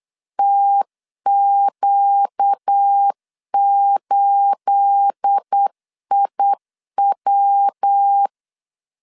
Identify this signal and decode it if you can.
Morse code, heard as a keyed tone, about 8 wpm, 800 Hz. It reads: TQ8IW
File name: 8wpm